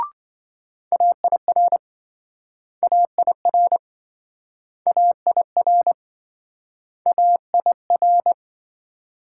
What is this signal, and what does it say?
Morse code, heard as a keyed tone, about 26 wpm, 700 Hz. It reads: AIR AIR AIR AIR